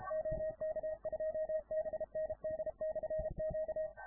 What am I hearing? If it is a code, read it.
Morse code, heard as a keyed tone, about 33 wpm, 630 Hz. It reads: OK2BND/Q